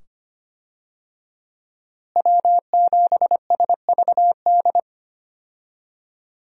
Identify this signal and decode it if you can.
Morse code, heard as a keyed tone, about 25 wpm, 700 Hz. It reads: W7SVD
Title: USA Call Signs - 5 Characters - Repeated 2x - 25wpm